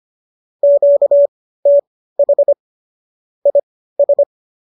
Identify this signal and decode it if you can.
Morse code, heard as a keyed tone, about 25 wpm, spaced as for 15 wpm, 570 Hz. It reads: QTH IS
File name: qso213 15